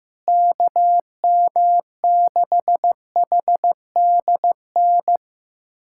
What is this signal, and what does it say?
Morse code, heard as a keyed tone, about 16 wpm, 700 Hz. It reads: KM6HDN
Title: USA Callsign DE Callsign - 6 Characters - 15wpm